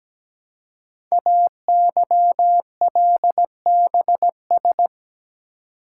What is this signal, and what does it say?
Morse code, heard as a keyed tone, about 17 wpm, 700 Hz. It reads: AYLBS
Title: Code Groups - 5 Letter:Numbers - 17wpm